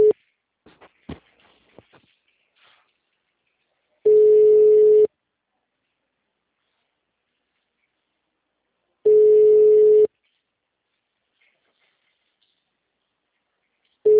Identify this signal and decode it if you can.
Ringback tone (Europe, 425 Hz).